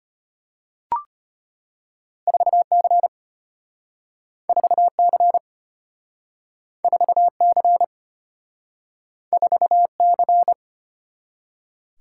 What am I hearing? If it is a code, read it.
Morse code, heard as a keyed tone, about 34 wpm, 700 Hz. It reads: 4C 4C 4C 4C